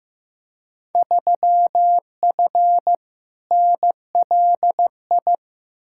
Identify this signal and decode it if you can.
Morse code, heard as a keyed tone, about 15 wpm, 700 Hz. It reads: 3F NLI
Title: Field Day Exchanges - 15wpm